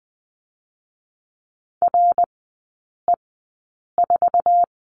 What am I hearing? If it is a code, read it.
Morse code, heard as a keyed tone, about 20 wpm, spaced as for 8 wpm, 700 Hz. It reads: RE4
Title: 081 Lesson 14e - Number 2 - Three Characters 20wpm